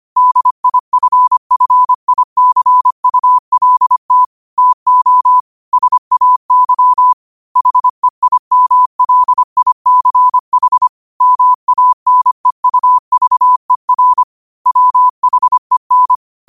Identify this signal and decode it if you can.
Morse code, heard as a keyed tone, about 25 wpm, 1000 Hz. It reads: DIFFICULT TO SAY HEIMLICH MANEUVER WHEN
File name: qotdfiles / 25wpm